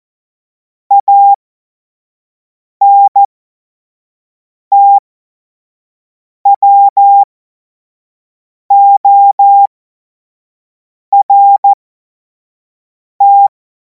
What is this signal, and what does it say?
Morse code, heard as a keyed tone, about 14 wpm, spaced as for 5 wpm, 800 Hz. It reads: ANTWORT
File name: Morsekurs / Witze / Witz 23. Juni 2026 25 F70